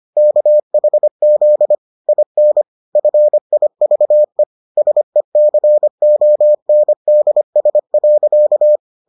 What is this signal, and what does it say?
Morse code, heard as a keyed tone, about 25 wpm, 600 Hz. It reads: KHZ IN FIVE SECONDS.